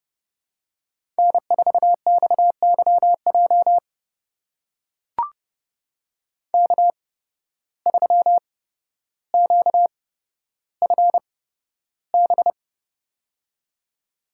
Morse code, heard as a keyed tone, about 32 wpm, spaced as for 13 wpm, 700 Hz. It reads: N4XYJ K3QFB
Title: USA Call Signs - 5 Characters - 8x Character Spacing - 30wpm